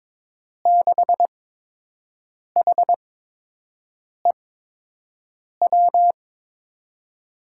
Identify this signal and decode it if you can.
Morse code, heard as a keyed tone, about 22 wpm, spaced as for 6 wpm, 700 Hz. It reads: 6HEW
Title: USA Call Signs - 8x Character Spacing - 22wpm